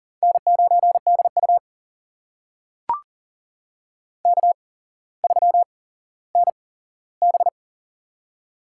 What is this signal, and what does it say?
Morse code, heard as a keyed tone, about 40 wpm, spaced as for 17 wpm, 700 Hz. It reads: N9DU K3NB
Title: USA Call Signs - 4 Characters - 8x Character Spacing - 40wpm